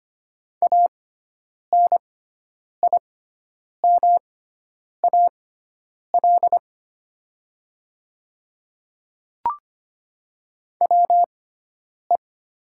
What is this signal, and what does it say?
Morse code, heard as a keyed tone, about 25 wpm, spaced as for 9 wpm, 700 Hz. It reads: ANIMAL WE
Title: 6x Character Spacing - Top 200 Words - 25wpm